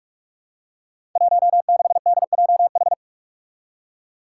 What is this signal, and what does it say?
Morse code, heard as a keyed tone, about 44 wpm, 700 Hz. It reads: 16DJH
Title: Five Challenging Letter-Number Combinations 45wpm